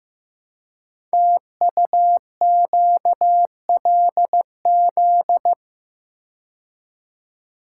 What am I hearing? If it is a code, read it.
Morse code, heard as a keyed tone, about 15 wpm, 700 Hz. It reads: TUQLZ
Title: Code Groups - 5 Letters - 15wpm